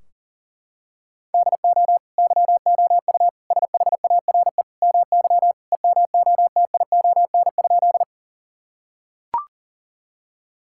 Morse code, heard as a keyed tone, about 40 wpm, 700 Hz. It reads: DO YOU SHARE MY EMOTION?
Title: Sentences from Top 1600 Words 40wpm